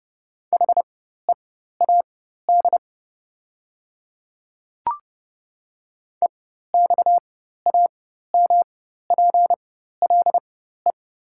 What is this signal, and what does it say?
Morse code, heard as a keyed tone, about 30 wpm, spaced as for 14 wpm, 690 Hz. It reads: HEAD EXAMPLE
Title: 4x Character Spacing - Top 500 Words - 30wpm